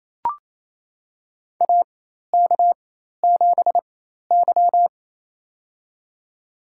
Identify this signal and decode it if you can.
Morse code, heard as a keyed tone, about 28 wpm, spaced as for 13 wpm, 700 Hz. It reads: AK7Y
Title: USA Call Signs - 4 Characters - 4x Character Spacing - 28wpm